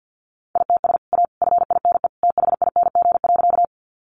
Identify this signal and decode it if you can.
Morse code, heard as a keyed tone, about 50 wpm, 700 Hz. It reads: ITS A FINE THING.
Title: Sentences from Top 600 Words 50wpm